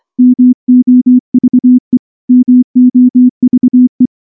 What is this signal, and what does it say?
Morse code, heard as a keyed tone, about 24 wpm, 250 Hz. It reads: MOVE MOVE
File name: morse / morse-training-25WPM